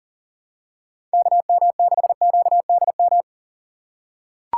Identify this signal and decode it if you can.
Morse code, heard as a keyed tone, about 40 wpm, 700 Hz. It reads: KM6QDM
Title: USA Call Signs - 6 Characters - 4x Character Spacing - 40wpm